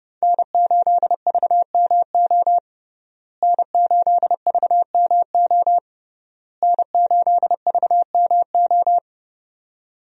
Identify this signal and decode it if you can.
Morse code, heard as a keyed tone, about 32 wpm, 700 Hz. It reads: N8VMO N8VMO N8VMO
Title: USA Call Signs - 5 Characters - Repeated 3x - 30wpm